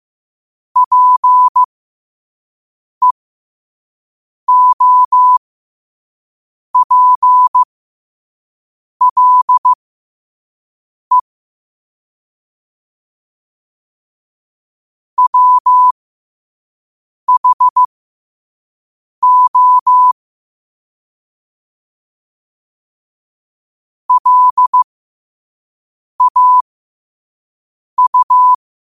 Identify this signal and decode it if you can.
Morse code, heard as a keyed tone, about 15 wpm, spaced as for 5 wpm, 1000 Hz. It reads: PEOPLE WHO LAU
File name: qotdfiles / 5wpm